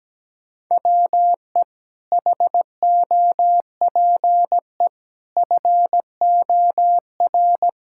Morse code, heard as a keyed tone, about 17 wpm, 700 Hz. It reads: WE HOPE FOR
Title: Sentences from Top 1000 Words - Review Part 6 - 17wpm